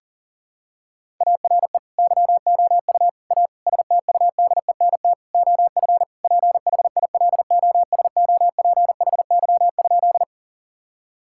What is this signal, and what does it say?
Morse code, heard as a keyed tone, about 40 wpm, 700 Hz. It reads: ARE YOU A STUDENT OF PHILOSOPHY?
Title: Sentences from Top 1000 Words - Review Part 6 - 40wpm